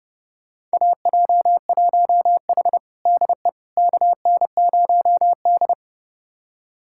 Morse code, heard as a keyed tone, about 30 wpm, 700 Hz. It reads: AJ1H DE KN0D
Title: USA Callsign DE Callsign - 4 Characters - 30wpm